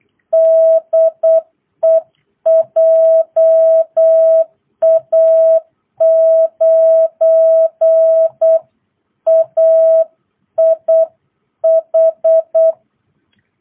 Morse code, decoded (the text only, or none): DE JA9AIH